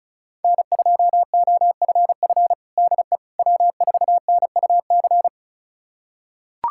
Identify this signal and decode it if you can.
Morse code, heard as a keyed tone, about 36 wpm, 700 Hz. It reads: N2OFF DE W4NUC
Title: USA Callsign DE Callsign - 5 Characters - 35wpm